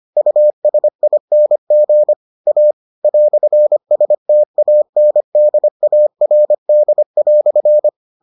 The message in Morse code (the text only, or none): USING A "STANDARD"